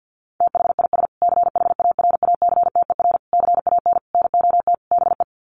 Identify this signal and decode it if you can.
Morse code, heard as a keyed tone, about 50 wpm, 700 Hz. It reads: THIS CHARACTER CAN NOT BE
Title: Sets of 5 Words Encore2 50wpm